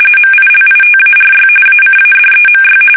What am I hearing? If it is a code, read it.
RTTY, about 45.45 baud. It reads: MILITARY ACADEMY
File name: fsk850 / usnavy2